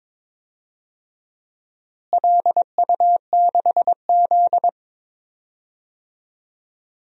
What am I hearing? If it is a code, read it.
Morse code, heard as a keyed tone, about 22 wpm, 700 Hz. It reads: LU6Z
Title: DXCC-100 Call Signs - 4 Characters - Repeated 2x - 22wpm